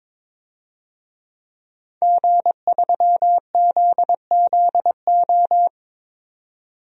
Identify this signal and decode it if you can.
Morse code, heard as a keyed tone, about 22 wpm, 700 Hz. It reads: G3ZZO
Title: DXCC-100 Call Signs - 5 Characters - 22wpm